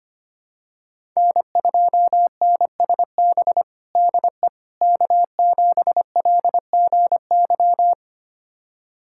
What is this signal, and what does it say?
Morse code, heard as a keyed tone, about 25 wpm, 700 Hz. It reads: N2NSB DE K7LGY
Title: USA Callsign DE Callsign - 5 Characters - 25wpm